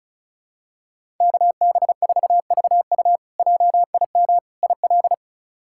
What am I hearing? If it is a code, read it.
Morse code, heard as a keyed tone, about 34 wpm, 700 Hz. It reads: KB4VU JIM IL
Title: K1USN SST QSO Practice - 35wpm